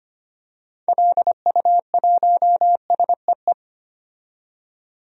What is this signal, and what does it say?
Morse code, heard as a keyed tone, about 25 wpm, 690 Hz. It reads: LU1SEE
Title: DXCC-100 Call Signs - 6 Characters - 25wpm